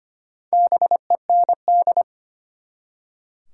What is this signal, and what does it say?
Morse code, heard as a keyed tone, about 25 wpm, 700 Hz. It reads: BEND
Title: Four Letter Words - 25wpm